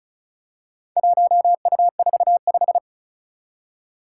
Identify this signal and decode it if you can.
Morse code, heard as a keyed tone, about 34 wpm, 700 Hz. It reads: 1U45
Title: Four Challenging Letter-Number Combinations 35wpm